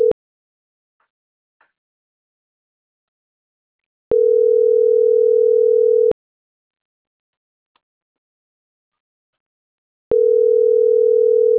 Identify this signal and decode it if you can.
Ringback tone (North America), 440 + 480 Hz.